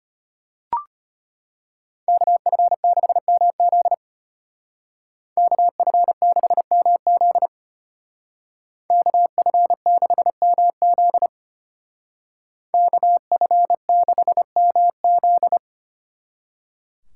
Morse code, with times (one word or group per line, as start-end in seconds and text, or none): 2.1-3.9 KF6MZ
5.4-7.5 KF6MZ
8.9-11.3 KF6MZ
12.7-15.6 KF6MZ